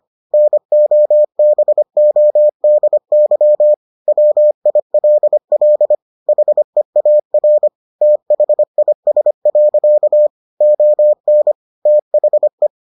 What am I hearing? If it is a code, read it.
Morse code, heard as a keyed tone, about 25 wpm, 600 Hz. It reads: NOBODY WILL HEAR THIS. ON THE